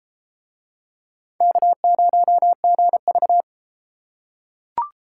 Morse code, heard as a keyed tone, about 31 wpm, 700 Hz. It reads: K0GV